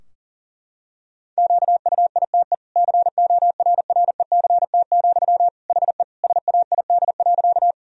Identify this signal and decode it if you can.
Morse code, heard as a keyed tone, about 40 wpm, 700 Hz. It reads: QUITE CORRECT, HE SAID.
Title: Sentences from Top 1000 Words - Review Part 2 - 40wpm